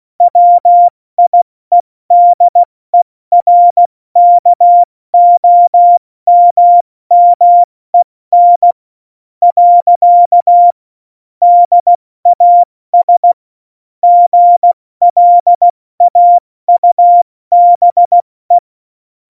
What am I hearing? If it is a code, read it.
Morse code, heard as a keyed tone, about 16 wpm, spaced as for 14 wpm, 700 Hz. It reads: WIEDERKOMMEN. DAS GLAUBE
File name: Morsekurs / Witze / Witz 24. November 2020, 70 F80